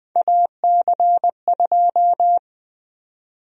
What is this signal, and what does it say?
Morse code, heard as a keyed tone, about 21 wpm, 700 Hz. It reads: AC2
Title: USA Call Sign Prefixes - 20wpm